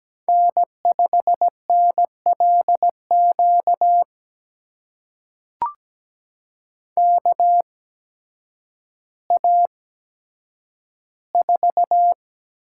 Morse code, heard as a keyed tone, about 17 wpm, spaced as for 9 wpm, 700 Hz. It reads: N5NLQ KA4
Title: USA Call Signs - 8x Character Spacing - 17wpm